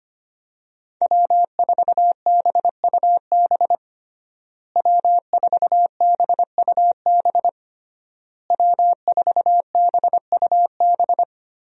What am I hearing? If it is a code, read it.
Morse code, heard as a keyed tone, about 25 wpm, 700 Hz. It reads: W4BUB W4BUB W4BUB
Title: USA Call Signs - 5 Characters - Repeated 3x - 25wpm